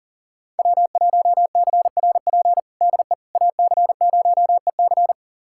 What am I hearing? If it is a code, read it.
Morse code, heard as a keyed tone, about 37 wpm, 700 Hz. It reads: W1CRP DE AC0EC
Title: USA Callsign DE Callsign - 5 Characters - 40wpm